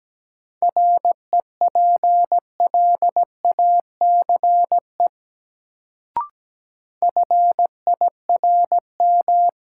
Morse code, heard as a keyed tone, about 17 wpm, 700 Hz. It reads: REPLACE FIRM
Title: Sentences from Top 1400 Words - New Word Drill - 17wpm